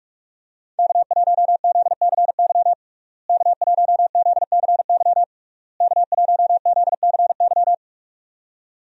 Morse code, heard as a keyed tone, about 46 wpm, 700 Hz. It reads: K1ZCY K1ZCY K1ZCY
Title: USA Call Signs - 5 Characters - Repeated 3x - 45wpm